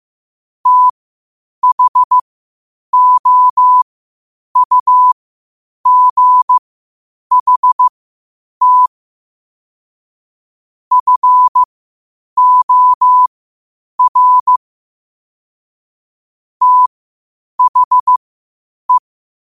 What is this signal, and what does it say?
Morse code, heard as a keyed tone, about 15 wpm, spaced as for 8 wpm, 1000 Hz. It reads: THOUGHT FOR THE